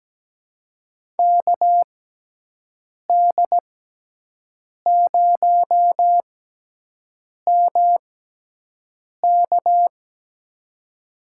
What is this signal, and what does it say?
Morse code, heard as a keyed tone, about 17 wpm, spaced as for 6 wpm, 700 Hz. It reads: KD0MK
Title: USA Call Signs - 5 Characters - 6x Character Spacing - 17wpm